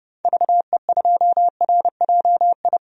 Morse code, heard as a keyed tone, about 32 wpm, 700 Hz. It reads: VE2RJI
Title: DXCC-100 Call Signs - 6 Characters - 30wpm